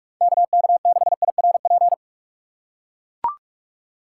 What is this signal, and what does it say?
Morse code, heard as a keyed tone, about 44 wpm, 700 Hz. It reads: KK6IRP
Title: USA Call Signs - 6 Characters - 2x Character Spacing - 45wpm